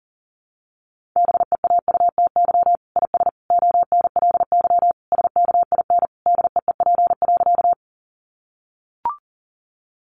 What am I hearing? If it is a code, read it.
Morse code, heard as a keyed tone, about 40 wpm, 700 Hz. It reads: BEAUTY IS ONLY SKIN DEEP.
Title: Sentences from Top 1300 Words 40wpm